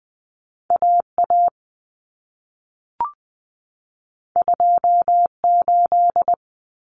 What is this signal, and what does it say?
Morse code, heard as a keyed tone, about 20 wpm, 700 Hz. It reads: AA 28
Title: Common Cut Numbers - 2-Digit 20wpm